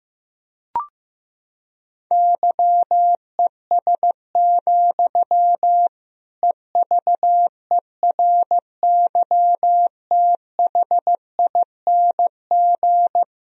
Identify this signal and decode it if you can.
Morse code, heard as a keyed tone, about 16 wpm, 700 Hz. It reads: YES, EVERYTHING
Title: Sentences from Top 1000 Words - Review Part 2 - 15wpm